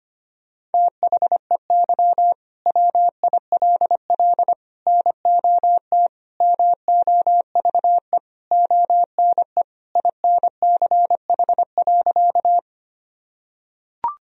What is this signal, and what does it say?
Morse code, heard as a keyed tone, about 25 wpm, 700 Hz. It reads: THEY WILL NOT MOVE ONE INCH.
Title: Sentences from Top 1000 Words - Review Part 3 - 25wpm